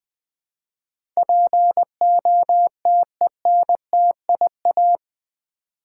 Morse code, heard as a keyed tone, about 20 wpm, 700 Hz. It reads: POTENTIA
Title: Build A Word - 4 Letters At A Time - 11-Letter Words - 20wpm